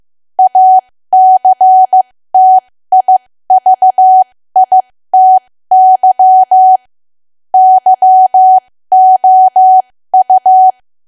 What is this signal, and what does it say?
Morse code, heard as a keyed tone, about 15 wpm, spaced as for 13 wpm, 750 Hz. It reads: ACTIVITY YOU
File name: cwa / W1AW-Level2 / 13WPM